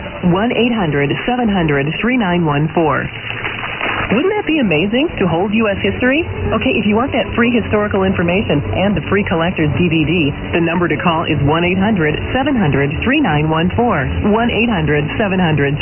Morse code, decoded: none

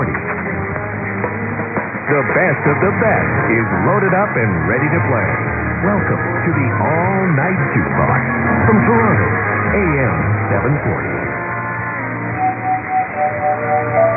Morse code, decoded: none

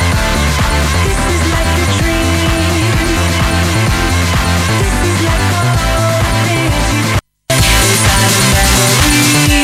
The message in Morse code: none

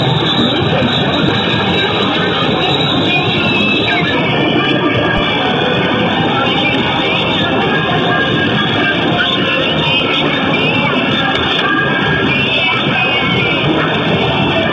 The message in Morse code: none

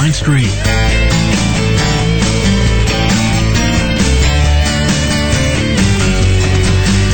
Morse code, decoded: none

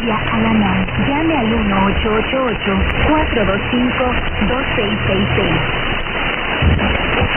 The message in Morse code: none

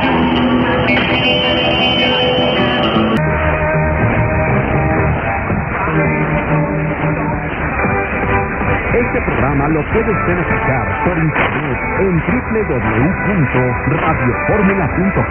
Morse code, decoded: none

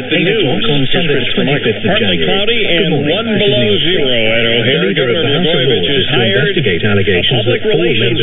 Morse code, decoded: none

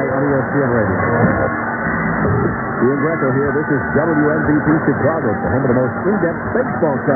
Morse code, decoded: none

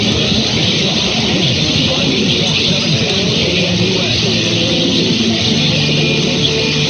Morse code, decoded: none

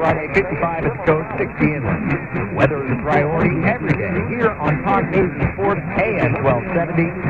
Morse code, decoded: none

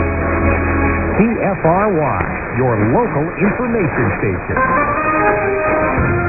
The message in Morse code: none

